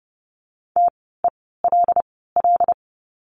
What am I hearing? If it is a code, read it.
Morse code, heard as a keyed tone, about 30 wpm, spaced as for 17 wpm, 700 Hz. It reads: TELL
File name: course / 30:10wpm / 115 Lesson 19f - Letter w - Words 30wpm